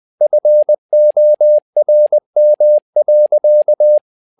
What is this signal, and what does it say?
Morse code, heard as a keyed tone, about 20 wpm, 600 Hz. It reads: FORM.
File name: learning center / tower trouble / tower trouble 20wpm